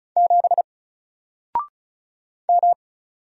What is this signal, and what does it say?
Morse code, heard as a keyed tone, about 35 wpm, 700 Hz. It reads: 7 M
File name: Single-Letter-Number-35wpm